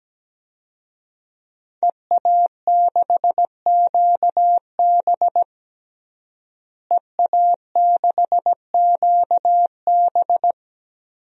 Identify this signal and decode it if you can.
Morse code, heard as a keyed tone, about 17 wpm, 700 Hz. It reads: EA6QB EA6QB
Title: DXCC-100 Call Signs - 5 Characters - Repeated 2x - 17wpm